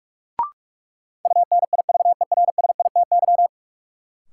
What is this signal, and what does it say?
Morse code, heard as a keyed tone, about 46 wpm, 700 Hz. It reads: UNIVERSITY
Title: Sentences from Top 1000 Words - Review Part 4 - New Word Drill - 45wpm